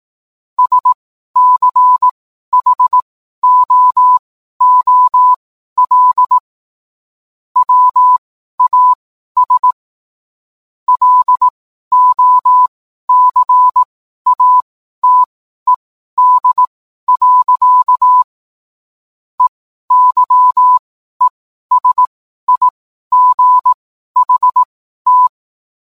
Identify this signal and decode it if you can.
Morse code, heard as a keyed tone, about 18 wpm, spaced as for 12 wpm, 1000 Hz. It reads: SCHOOL WAS LOCATED. EYESIGHT
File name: cwblog / cw mp3 directory / mem morse 12